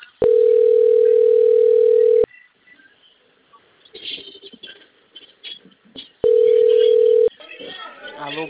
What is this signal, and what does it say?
Ringback tone (North America), 440 + 480 Hz, with speech.
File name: rg-1600-unknown- - - .24905